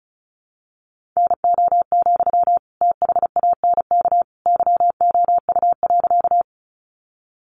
Morse code, heard as a keyed tone, about 34 wpm, 700 Hz. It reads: NO, THANK YOU.
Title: Sentences from Top 600 Words 35wpm